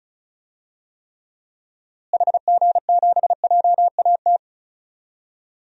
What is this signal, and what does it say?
Morse code, heard as a keyed tone, about 36 wpm, 700 Hz. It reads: HG7JAT
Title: DXCC-100 Call Signs - 6 Characters - Repeated 2x - 35wpm